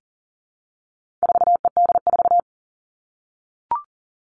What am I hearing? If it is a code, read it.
Morse code, heard as a keyed tone, about 40 wpm, 700 Hz. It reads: 4ED4
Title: All Cut Numbers - 4-Digit 40wpm